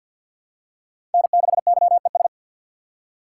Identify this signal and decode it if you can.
Morse code, heard as a keyed tone, about 52 wpm, 700 Hz. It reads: N6YES
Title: USA Call Signs - 5 Characters - 4x Character Spacing - 50wpm